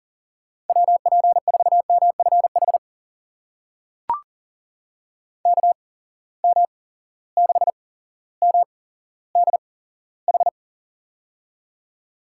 Morse code, heard as a keyed tone, about 39 wpm, spaced as for 17 wpm, 700 Hz. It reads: WP4MFH KM6MDH
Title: USA Call Signs - 6 Characters - 8x Character Spacing - 40wpm